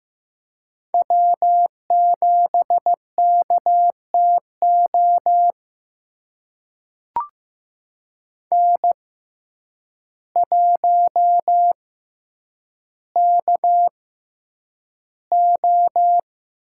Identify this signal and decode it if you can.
Morse code, heard as a keyed tone, about 15 wpm, spaced as for 8 wpm, 700 Hz. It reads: W7KTO N1KO
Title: USA Call Signs - 5 Characters - 6x Character Spacing - 15wpm